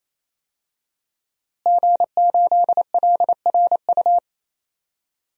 Morse code, heard as a keyed tone, about 27 wpm, 700 Hz. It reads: G8LRU
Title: DXCC-100 Call Signs - 5 Characters - Repeated 3x - 28wpm